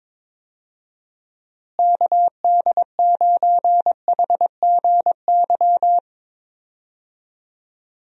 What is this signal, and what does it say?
Morse code, heard as a keyed tone, about 21 wpm, 700 Hz. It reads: KD9HGY
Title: USA Call Signs - 6 Characters - Repeated 2x - 22wpm